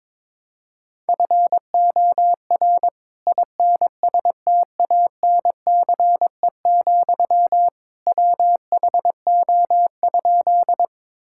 Morse code, heard as a keyed tone, about 22 wpm, 700 Hz. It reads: FOR INSTANCE, WHO?
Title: Sentences from Top 1000 Words - Review Part 5 - 22wpm